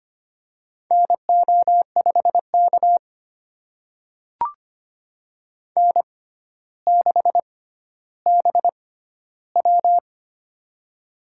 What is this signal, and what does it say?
Morse code, heard as a keyed tone, about 25 wpm, spaced as for 13 wpm, 700 Hz. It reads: NO5K N6BW
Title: USA Call Signs - 4 Characters - 6x Character Spacing - 25wpm